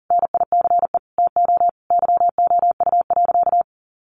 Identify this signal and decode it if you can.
Morse code, heard as a keyed tone, about 40 wpm, 700 Hz. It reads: NICE TO YOU.
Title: Sentences from Top 1100 Words 40wpm